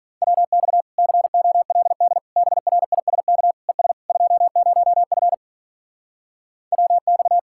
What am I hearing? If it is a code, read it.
Morse code, heard as a keyed tone, about 45 wpm, 700 Hz. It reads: WX COLD BRISK ES 20F WX